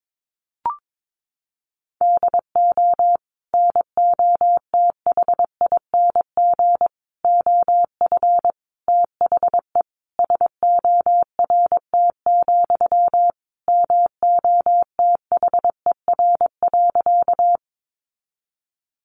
Morse code, heard as a keyed tone, about 22 wpm, 700 Hz. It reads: DO NOTHING OF THE SORT, MOTHER.
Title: Sentences from Top 1300 Words 22wpm